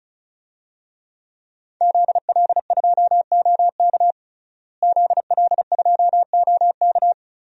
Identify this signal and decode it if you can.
Morse code, heard as a keyed tone, about 34 wpm, 700 Hz. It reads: ZL2OK ZL2OK
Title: DXCC-100 Call Signs - 5 Characters - Repeated 3x - 35wpm